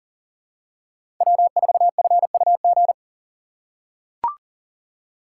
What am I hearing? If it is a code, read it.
Morse code, heard as a keyed tone, about 40 wpm, 700 Hz. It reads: W4FUG
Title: USA Call Signs - 5 Characters - 4x Character Spacing - 40wpm